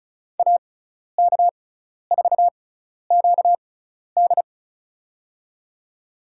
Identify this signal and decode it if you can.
Morse code, heard as a keyed tone, about 36 wpm, spaced as for 12 wpm, 700 Hz. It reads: AK4QD
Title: USA Call Signs - 6x Character Spacing - 35wpm